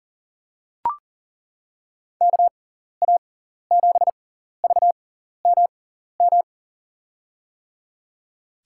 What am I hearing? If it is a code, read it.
Morse code, heard as a keyed tone, about 40 wpm, spaced as for 14 wpm, 700 Hz. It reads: KA7VMM
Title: USA Call Signs - 6x Character Spacing - 40wpm